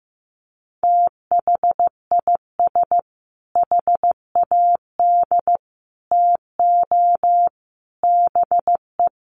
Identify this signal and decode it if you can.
Morse code, heard as a keyed tone, about 15 wpm, 700 Hz. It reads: THIS HAD TO BE
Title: Sentences from Top 1300 Words 15wpm